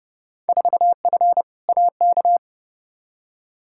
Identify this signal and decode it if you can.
Morse code, heard as a keyed tone, about 30 wpm, 700 Hz. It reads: 4F AK
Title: Field Day Exchanges Encore - 30wpm